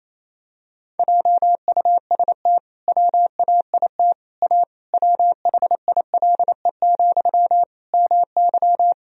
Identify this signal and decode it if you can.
Morse code, heard as a keyed tone, about 27 wpm, 700 Hz. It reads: JUST WAIT A WHILE, MY
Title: Sentences from Top 1000 Words - Review Part 1 - 28wpm